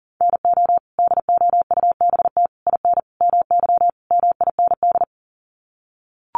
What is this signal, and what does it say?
Morse code, heard as a keyed tone, about 40 wpm, 700 Hz. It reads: NO DOUBT IN MY MIND
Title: Sets of 5 Words Encore2 40wpm